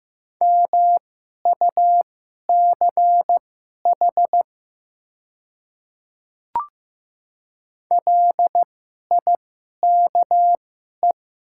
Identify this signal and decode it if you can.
Morse code, heard as a keyed tone, about 15 wpm, spaced as for 11 wpm, 700 Hz. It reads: MUCH LIKE
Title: 2x Character Spacing - Top 500 Words - 15wpm